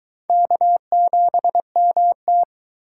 Morse code, heard as a keyed tone, about 20 wpm, 700 Hz. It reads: K7MT